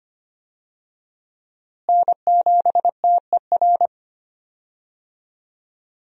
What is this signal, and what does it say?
Morse code, heard as a keyed tone, about 25 wpm, 700 Hz. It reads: N7TER
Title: USA Call Signs - 5 Characters - Repeated 3x - 25wpm